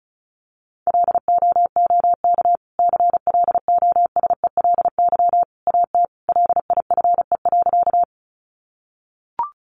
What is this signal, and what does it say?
Morse code, heard as a keyed tone, about 34 wpm, 700 Hz. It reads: LOOK CLOSELY AT LIFE.
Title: Sentences from Top 1400 Words 35wpm